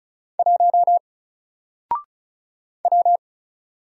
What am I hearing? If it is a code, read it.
Morse code, heard as a keyed tone, about 36 wpm, 700 Hz. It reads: 1 W
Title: Single Letter-Number Alt 35wpm